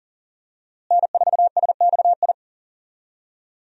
Morse code, heard as a keyed tone, about 40 wpm, 700 Hz. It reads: N4SXI
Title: USA Call Signs - 5 Characters - Repeated 3x - 40wpm